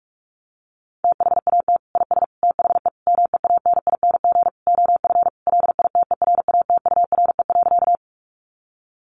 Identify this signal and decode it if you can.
Morse code, heard as a keyed tone, about 44 wpm, 700 Hz. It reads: THAT IS THE MEANING OF LITERATURE.
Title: Sentences from Top 900 Words 45wpm